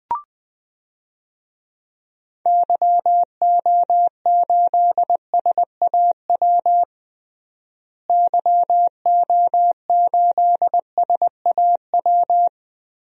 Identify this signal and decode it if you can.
Morse code, heard as a keyed tone, about 21 wpm, 700 Hz. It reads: YO8SAW YO8SAW
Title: DXCC-100 Call Signs - 6 Characters - Repeated 3x - 20wpm